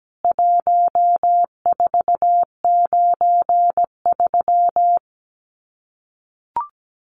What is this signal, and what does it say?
Morse code, heard as a keyed tone, about 17 wpm, 700 Hz. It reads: 1493
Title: CWT Exchange 17wpm